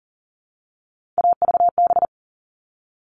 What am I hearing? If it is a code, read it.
Morse code, heard as a keyed tone, about 40 wpm, 700 Hz. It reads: AVB
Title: All Cut Numbers - 3-Digit 40wpm